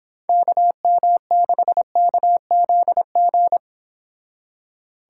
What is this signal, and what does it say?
Morse code, heard as a keyed tone, about 29 wpm, 700 Hz. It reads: KM6KZG